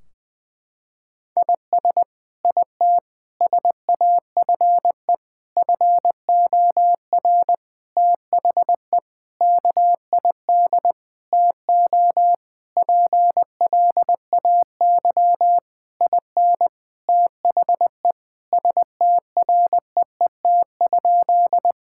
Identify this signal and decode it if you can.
Morse code, heard as a keyed tone, about 21 wpm, 700 Hz. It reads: IS IT SAFE FOR THE KID TO PLAY IN THE STREET?
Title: Sentences from Top 1600 Words 20wpm